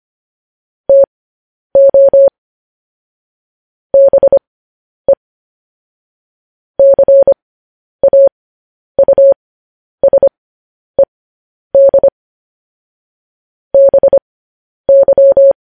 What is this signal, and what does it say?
Morse code, heard as a keyed tone, about 25 wpm, spaced as for 10 wpm, 550 Hz. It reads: TO BE CAUSED BY
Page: news-headlines-in-morse-code-at-10-wpm